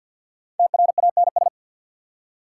Morse code, heard as a keyed tone, about 50 wpm, 690 Hz. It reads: TRANS
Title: Prefixes and Suffixes 50wpm